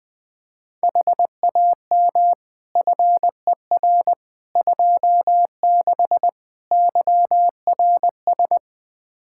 Morse code, heard as a keyed tone, about 20 wpm, 700 Hz. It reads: HAM FER 26 YRS
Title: Practice QSO - Exchange 3 - More Info - 20wpm